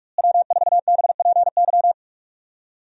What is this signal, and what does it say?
Morse code, heard as a keyed tone, about 46 wpm, 700 Hz. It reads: W4BPY
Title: USA Call Signs - 5 Characters - Repeated 3x - 45wpm